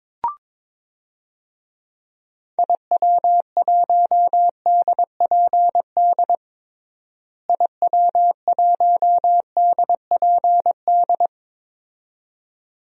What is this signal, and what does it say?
Morse code, heard as a keyed tone, about 22 wpm, 700 Hz. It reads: IW1DPD IW1DPD
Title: DXCC-100 Call Signs - 6 Characters - Repeated 2x - 22wpm